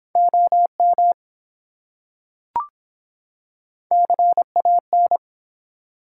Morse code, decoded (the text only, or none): OM CAN